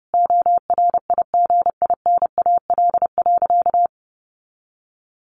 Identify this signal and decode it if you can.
Morse code, heard as a keyed tone, about 30 wpm, 700 Hz. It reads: ORIGINAL.
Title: Sentences from Top 900 Words 30wpm